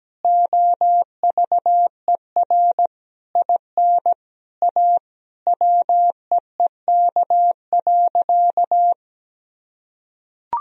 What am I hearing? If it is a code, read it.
Morse code, heard as a keyed tone, about 17 wpm, 700 Hz. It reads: OVER IN A WEEK.
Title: Sentences from Top 1000 Words - Review Part 2 - 17wpm